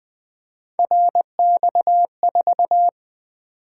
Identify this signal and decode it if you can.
Morse code, heard as a keyed tone, about 20 wpm, 700 Hz. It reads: RX4
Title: Three Letter-Number Combinations - 20wpm